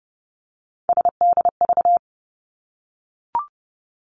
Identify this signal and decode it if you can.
Morse code, heard as a keyed tone, about 30 wpm, 700 Hz. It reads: SDV
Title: Three Challenging Letter-Number Combinations 30wpm